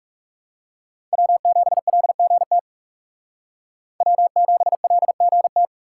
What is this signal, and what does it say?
Morse code, heard as a keyed tone, about 37 wpm, 700 Hz. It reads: W7LGT W7LGT
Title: Speed Racing - USA 1x3 Call Signs 30wpm